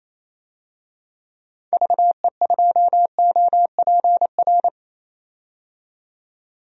Morse code, heard as a keyed tone, about 28 wpm, 700 Hz. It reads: VE2OPR
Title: DXCC-100 Call Signs - 6 Characters - 28wpm